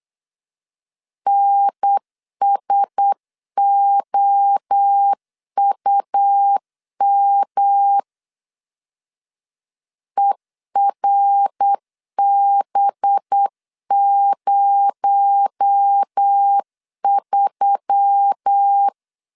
Morse code, heard as a keyed tone, about 8 wpm, 800 Hz. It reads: NSOUM ERB03
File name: LEZIONI / 6 PKX0 / 6 PKX0 8wpm 3341